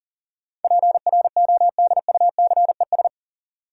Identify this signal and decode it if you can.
Morse code, heard as a keyed tone, about 40 wpm, 700 Hz. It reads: PRODUCES
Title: Sentences from Top 1000 Words - Review Part 6 - New Word Drill - 40wpm